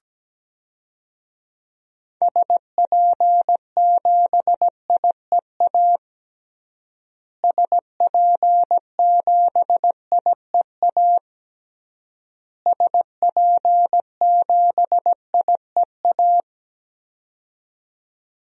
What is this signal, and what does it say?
Morse code, heard as a keyed tone, about 17 wpm, 700 Hz. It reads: SP7IEA SP7IEA SP7IEA
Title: DXCC-100 Call Signs - 6 Characters - Repeated 3x - 17wpm